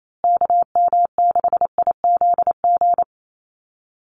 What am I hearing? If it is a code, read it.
Morse code, heard as a keyed tone, about 28 wpm, 700 Hz. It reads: KM6IZG
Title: USA Call Signs Repeated 3x 28wpm